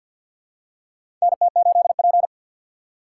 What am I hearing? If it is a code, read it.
Morse code, heard as a keyed tone, about 53 wpm, 700 Hz. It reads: NT8P
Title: USA Call Signs - 4x Character Spacing - 50wpm